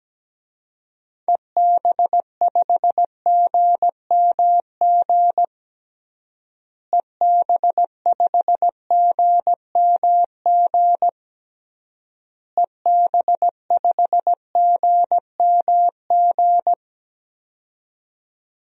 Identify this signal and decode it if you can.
Morse code, heard as a keyed tone, about 17 wpm, 700 Hz. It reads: EB5GMG EB5GMG EB5GMG
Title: DXCC-100 Call Signs - 6 Characters - Repeated 3x - 17wpm